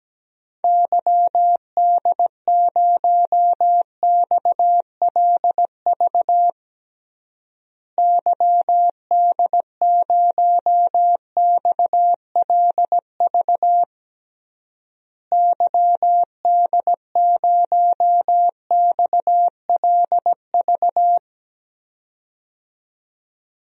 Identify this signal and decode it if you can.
Morse code, heard as a keyed tone, about 17 wpm, 700 Hz. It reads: YD0XLV YD0XLV YD0XLV
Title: DXCC-100 Call Signs - 6 Characters - Repeated 3x - 17wpm